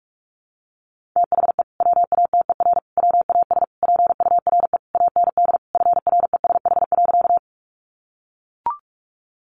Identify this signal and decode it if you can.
Morse code, heard as a keyed tone, about 44 wpm, 700 Hz. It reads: THE WATER WAS PURE AND FRESH.
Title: Sentences from Top 1500 Words 45wpm